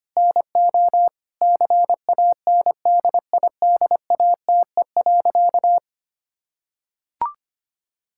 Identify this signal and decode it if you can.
Morse code, heard as a keyed tone, about 25 wpm, 700 Hz. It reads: NO CANDIDATE.
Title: Sentences from Top 1600 Words 25wpm